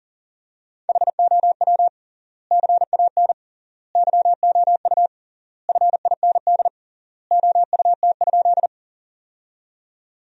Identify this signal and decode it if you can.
Morse code, heard as a keyed tone, about 40 wpm, 700 Hz. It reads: HOW CAN YOU FIND OUT?
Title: 3x Word-Spacing - Sentences from Top 200 Words - 40wpm